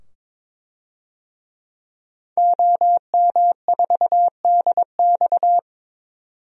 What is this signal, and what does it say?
Morse code, heard as a keyed tone, about 22 wpm, 700 Hz. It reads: OM4DX
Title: DXCC-100 Call Signs - 5 Characters - 22wpm